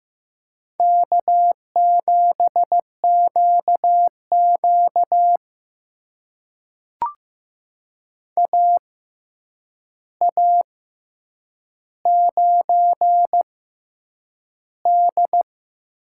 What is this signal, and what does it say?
Morse code, heard as a keyed tone, about 15 wpm, spaced as for 8 wpm, 700 Hz. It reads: K7QQ AA9D
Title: USA Call Signs - 4 Characters - 6x Character Spacing - 15wpm